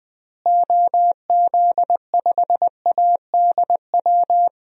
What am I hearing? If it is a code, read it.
Morse code, heard as a keyed tone, about 21 wpm, 700 Hz. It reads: OZ5ADW
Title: DXCC-100 Call Signs - 6 Characters - Repeated 3x - 20wpm